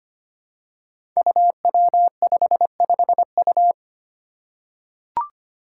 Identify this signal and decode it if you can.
Morse code, heard as a keyed tone, about 25 wpm, 700 Hz. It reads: UW55U
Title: Five Challenging Letter-Number Combinations 25wpm